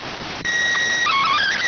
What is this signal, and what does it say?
SSTV (slow-scan television), PD 290.